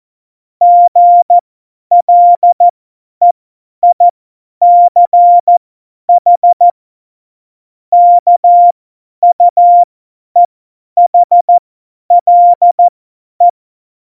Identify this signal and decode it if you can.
Morse code, heard as a keyed tone, about 14 wpm, spaced as for 10 wpm, 700 Hz. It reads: GLEICH KUEHLE
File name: Morsekurs / Witze / Witz 13. Februar 2018, 50 F70